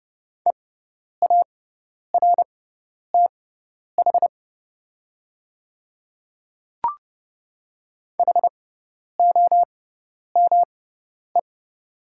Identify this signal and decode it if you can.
Morse code, heard as a keyed tone, about 30 wpm, spaced as for 10 wpm, 690 Hz. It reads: EARTH HOME
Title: 6x Character Spacing - Top 200 Words - 30wpm